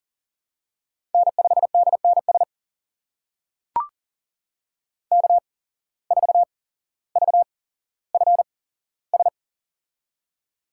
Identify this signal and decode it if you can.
Morse code, heard as a keyed tone, about 40 wpm, spaced as for 17 wpm, 700 Hz. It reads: N5DNS K4VFS
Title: USA Call Signs - 5 Characters - 8x Character Spacing - 40wpm